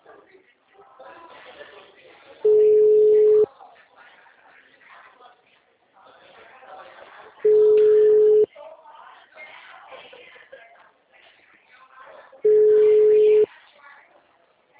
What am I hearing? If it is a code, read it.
Ringback tone (Europe, 425 Hz), with speech.
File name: out- -2012- - - .8